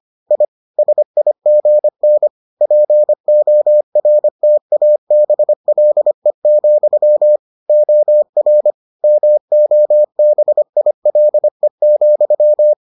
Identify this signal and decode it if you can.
Morse code, heard as a keyed tone, about 25 wpm, 600 Hz. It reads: I SIGN PORTABLE, OR MOBILE,